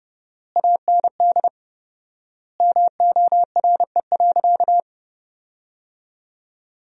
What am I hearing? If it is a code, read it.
Morse code, heard as a keyed tone, about 30 wpm, 700 Hz. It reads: AND MORE.
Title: 4x Word-Spacing - Sentences from Top 500 Words - 30wpm